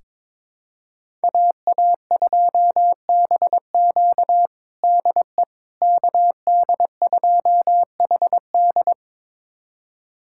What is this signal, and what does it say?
Morse code, heard as a keyed tone, about 21 wpm, 700 Hz. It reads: AA2BQ DE KD2HD
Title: USA Callsign DE Callsign - 5 Characters - 22wpm